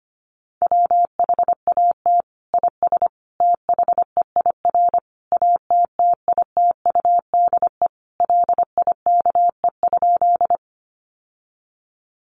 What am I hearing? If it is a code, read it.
Morse code, heard as a keyed tone, about 25 wpm, 700 Hz. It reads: WHAT IS THEIR ATTITUDE LIKE?
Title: Sentences from Top 1300 Words 25wpm